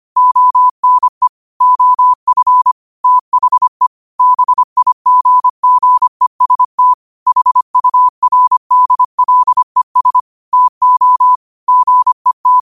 Morse code, heard as a keyed tone, about 25 wpm, 1000 Hz. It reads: ONE OF THE BIGGEST HURDLES TO GET
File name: qotdfiles / 25wpm